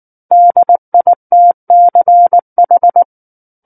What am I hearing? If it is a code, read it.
Morse code, heard as a keyed tone, about 19 wpm, 700 Hz. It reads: DITCH
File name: Book One / 10-Chapter 10 - 19 WPM